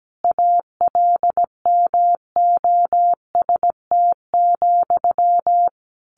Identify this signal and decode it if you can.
Morse code, heard as a keyed tone, about 17 wpm, 700 Hz. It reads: ALMOST,
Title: A Sound Of Thunder 17wpm